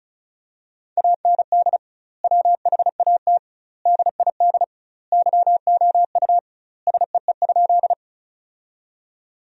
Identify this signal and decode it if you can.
Morse code, heard as a keyed tone, about 36 wpm, 700 Hz. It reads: AND WHAT DID YOU SEE?
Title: 2x Word-Spacing - Sentences from Top 100 Words - 35wpm